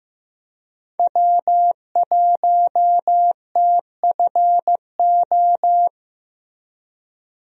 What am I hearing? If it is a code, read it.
Morse code, heard as a keyed tone, about 15 wpm, 700 Hz. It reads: W1TFO
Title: USA Call Signs - 5 Characters - Repeated 2x - 15wpm